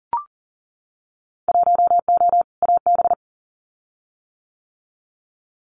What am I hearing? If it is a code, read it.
Morse code, heard as a keyed tone, about 41 wpm, 700 Hz. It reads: 1O AB
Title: Winter Field Day Exchanges 40wpm